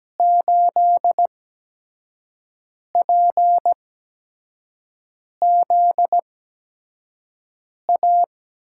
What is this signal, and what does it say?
Morse code, heard as a keyed tone, about 17 wpm, spaced as for 5 wpm, 700 Hz. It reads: 8PZA
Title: USA Call Signs - 6 Characters - 8x Character Spacing - 17wpm